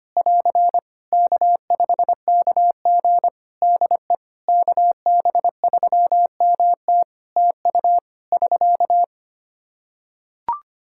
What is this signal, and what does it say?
Morse code, heard as a keyed tone, about 25 wpm, 700 Hz. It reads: <AR> K5KG DE KB3MT TU <SK>
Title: K1USN SST QSO Practice 25wpm